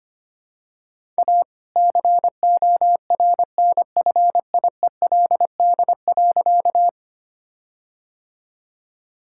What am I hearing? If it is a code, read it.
Morse code, heard as a keyed tone, about 25 wpm, 700 Hz. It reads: A CORNFIELD.
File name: practice / 25wpm / All-Age Appropriate Jokes - 25wpm